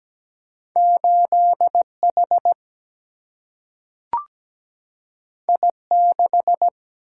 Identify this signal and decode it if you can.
Morse code, heard as a keyed tone, about 17 wpm, 700 Hz. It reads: 8 H I 6
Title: Single-Letter Single-Number 17wpm